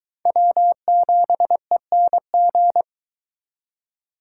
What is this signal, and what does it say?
Morse code, heard as a keyed tone, about 23 wpm, 700 Hz. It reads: W7ENG